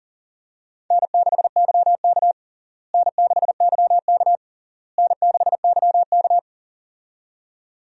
Morse code, heard as a keyed tone, about 37 wpm, 700 Hz. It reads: N6YK N6YK N6YK
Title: USA Call Signs - 4 Characters - Repeated 3x - 40wpm